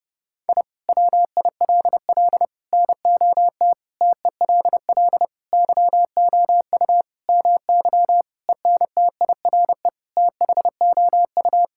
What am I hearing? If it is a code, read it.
Morse code, heard as a keyed tone, about 30 wpm, 700 Hz. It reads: I WILL NOT TELL YOU MY ENTIRE THOU
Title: Sentences from Top 1000 Words - Review Part 5 - 30wpm